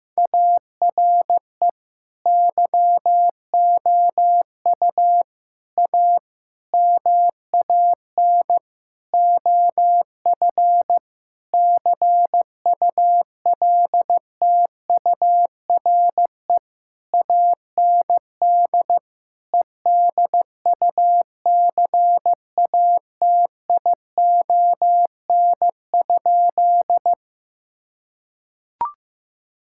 Morse code, heard as a keyed tone, about 15 wpm, 700 Hz. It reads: ARE YOU A MAN OF CULTURE AND EDUCATION?
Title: Sentences from Top 1000 Words - Review Part 6 - 15wpm